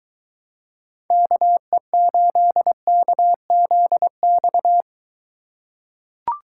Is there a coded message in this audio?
Yes: Morse code from 1.1 to 4.8 s, heard as a keyed tone, about 22 wpm, 700 Hz. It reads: KE8KZX